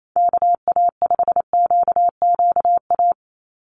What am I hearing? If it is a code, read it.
Morse code, heard as a keyed tone, about 28 wpm, 700 Hz. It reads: KA5QQA
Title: NA SOTA Chaser Callsigns 28wpm